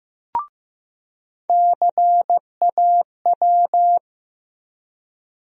Morse code, heard as a keyed tone, about 15 wpm, 700 Hz. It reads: CAW
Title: USA Call Sign Postfixes - 3 Letters 15wpm